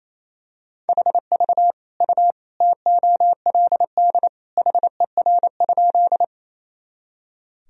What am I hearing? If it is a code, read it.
Morse code, heard as a keyed tone, about 28 wpm, 700 Hz. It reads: HV U TOLD HER?
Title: Sentences from Top 400 Words and Abbreviations 28wpm